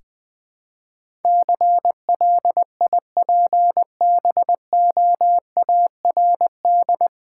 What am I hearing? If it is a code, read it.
Morse code, heard as a keyed tone, about 21 wpm, 700 Hz. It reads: CLIPBOARD
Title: Easy Words with Hints - 20wpm